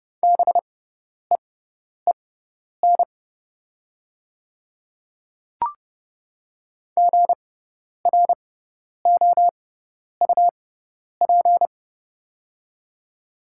Morse code, heard as a keyed tone, about 30 wpm, spaced as for 10 wpm, 700 Hz. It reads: BEEN GROUP
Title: 6x Character Spacing - Top 300 Words - 30wpm